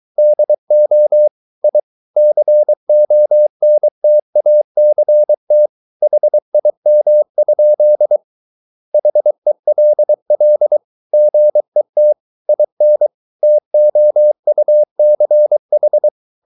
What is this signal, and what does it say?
Morse code, heard as a keyed tone, about 23 wpm, 600 Hz. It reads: DO I CONTACT HIM? HELL GET IN TOUCH